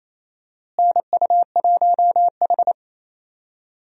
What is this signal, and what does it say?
Morse code, heard as a keyed tone, about 27 wpm, 700 Hz. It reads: NU1H
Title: USA Call Signs - 4 Characters - Repeated 3x - 28wpm